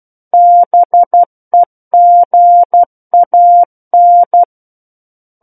Morse code, heard as a keyed tone, about 12 wpm, 700 Hz. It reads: BEGAN
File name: Book One / 03-Chapter 3 - 12 WPM